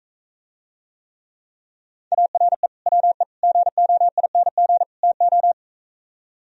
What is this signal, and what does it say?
Morse code, heard as a keyed tone, about 45 wpm, 700 Hz. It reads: ARE WE GOING TO